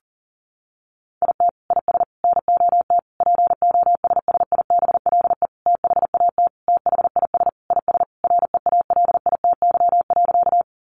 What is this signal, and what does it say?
Morse code, heard as a keyed tone, about 40 wpm, 700 Hz. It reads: IT IS NOT POSSIBLE THAT THIS IS REALITY.
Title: Sentences from Top 1200 Words 40wpm